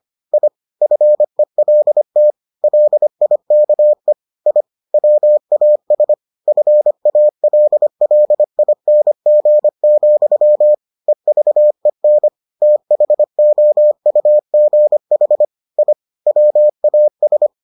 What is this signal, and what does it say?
Morse code, heard as a keyed tone, about 25 wpm, 600 Hz. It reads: I FELT LIKE I WAS FALLING, EVEN THOUGH I WAS